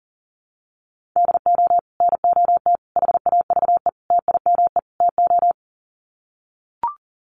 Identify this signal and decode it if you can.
Morse code, heard as a keyed tone, about 40 wpm, 700 Hz. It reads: DO NOT HAVE TIME TO
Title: Sets of 5 Words 40wpm